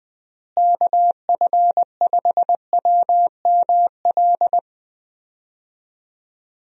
Morse code, heard as a keyed tone, about 20 wpm, 700 Hz. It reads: KF5WML
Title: USA Call Signs - 6 Characters - Repeated 3x - 20wpm